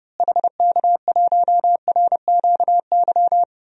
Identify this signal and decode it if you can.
Morse code, heard as a keyed tone, about 31 wpm, 700 Hz. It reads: HK1RQY
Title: DXCC-100 Call Signs - 6 Characters - Repeated 2x - 30wpm